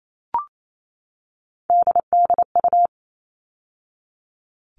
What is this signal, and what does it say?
Morse code, heard as a keyed tone, about 28 wpm, 700 Hz. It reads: DDU
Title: All Cut Numbers - 3-Digit 28wpm